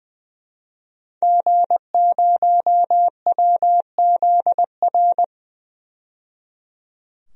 Morse code, heard as a keyed tone, about 20 wpm, 700 Hz. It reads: G0WZR
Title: DXCC-100 Call Signs - 5 Characters - 20wpm